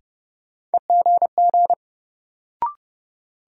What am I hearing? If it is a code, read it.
Morse code, heard as a keyed tone, about 30 wpm, 700 Hz. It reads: EGG
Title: Sentences from Top 1000 Words - Review Part 7 - New Word Drill - 30wpm